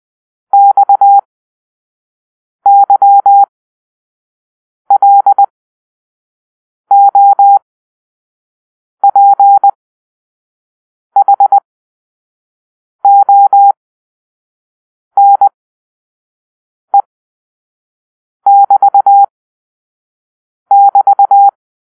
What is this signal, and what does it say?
Morse code, heard as a keyed tone, about 20 wpm, spaced as for 5 wpm, 800 Hz. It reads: XYLOPHONE<BT><BT>
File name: Grp-5 Drill-2